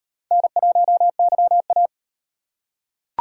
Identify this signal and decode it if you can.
Morse code, heard as a keyed tone, about 36 wpm, 700 Hz. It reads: N1YA